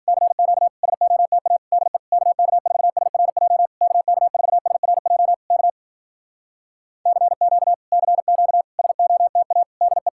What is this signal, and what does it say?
Morse code, heard as a keyed tone, about 56 wpm, 700 Hz. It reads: CQ SOTA DE KK4SRJ KK4SRJ K CQ CQ SOTA DE